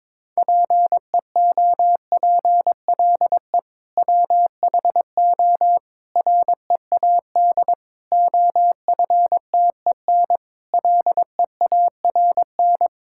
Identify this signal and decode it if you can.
Morse code, heard as a keyed tone, about 22 wpm, 700 Hz. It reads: PEOPLE WHO READ OFTEN LEARN
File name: practice / 22wpm / Sentences with 2-Word Phrases - 22wpm